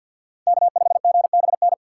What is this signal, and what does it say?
Morse code, heard as a keyed tone, about 57 wpm, 700 Hz. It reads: K5GBN
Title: Speed Racing - USA 1x3 Call Signs 50wpm